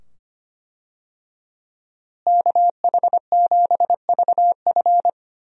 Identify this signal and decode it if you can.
Morse code, heard as a keyed tone, about 25 wpm, 700 Hz. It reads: KH7VF
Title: USA Call Signs - 5 Characters - Repeated 2x - 25wpm